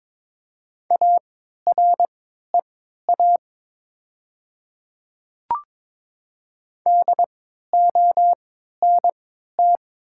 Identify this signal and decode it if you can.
Morse code, heard as a keyed tone, about 22 wpm, spaced as for 12 wpm, 700 Hz. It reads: AREA DONT
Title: 3x Character Spacing - Top 500 Words - 22wpm